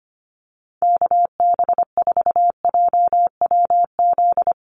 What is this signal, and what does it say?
Morse code, heard as a keyed tone, about 25 wpm, 700 Hz. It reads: KB4JWZ
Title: USA Call Signs Repeated 2x 25wpm